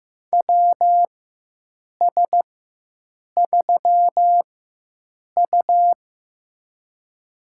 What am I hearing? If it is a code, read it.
Morse code, heard as a keyed tone, about 15 wpm, spaced as for 7 wpm, 700 Hz. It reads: WS3U
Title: USA Call Signs - 4 Characters - 4x Character Spacing - 15wpm